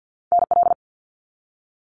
Morse code, heard as a keyed tone, about 50 wpm, 700 Hz. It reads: NL
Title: Canadian Provinces and Territories 50wpm